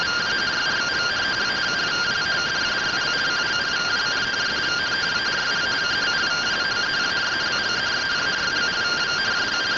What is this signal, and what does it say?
Two-tone teleprinter signal, about 170 Hz shift.
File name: findings / MR AXIAL 33,99mm 1.gen-44 rtty rx